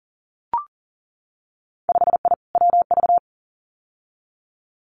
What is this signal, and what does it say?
Morse code, heard as a keyed tone, about 40 wpm, 700 Hz. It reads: 5I WV
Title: Winter Field Day Exchanges Encore 40wpm